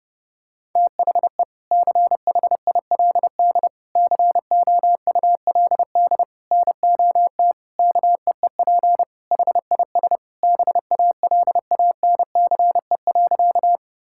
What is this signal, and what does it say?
Morse code, heard as a keyed tone, about 32 wpm, 700 Hz. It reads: THE CHILD COULD NOT KEEP HIS BALANCE.
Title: Sentences from Top 1000 Words - Review Part 6 - 30wpm